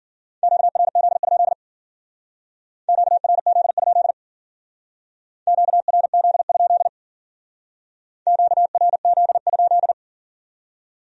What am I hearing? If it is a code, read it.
Morse code, heard as a keyed tone, about 53 wpm, 700 Hz. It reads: QRZ? QRZ? QRZ? QRZ?